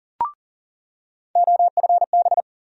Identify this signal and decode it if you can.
Morse code, heard as a keyed tone, about 40 wpm, 700 Hz. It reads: OFB
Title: USA Call Sign Postfixes - 3 Letters 40wpm